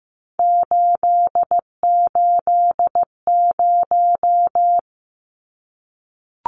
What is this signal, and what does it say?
Morse code, heard as a keyed tone, about 15 wpm, 700 Hz. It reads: 880
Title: Numbers - 3-Digit 15wpm